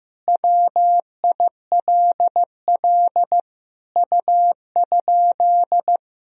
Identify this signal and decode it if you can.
Morse code, heard as a keyed tone, about 16 wpm, 700 Hz. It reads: WILL U?
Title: Sentences from Top 300 Words and Abbreviations 15wpm